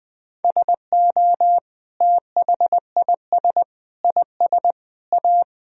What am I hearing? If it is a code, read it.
Morse code, heard as a keyed tone, about 20 wpm, 700 Hz. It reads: SO THIS IS A
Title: Sentences from Top 1000 Words - Review Part 1 - 20wpm